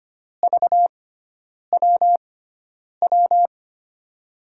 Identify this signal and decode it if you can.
Morse code, heard as a keyed tone, about 25 wpm, spaced as for 9 wpm, 700 Hz. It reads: VWW
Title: USA Call Signs - 6 Characters - 6x Character Spacing - 25wpm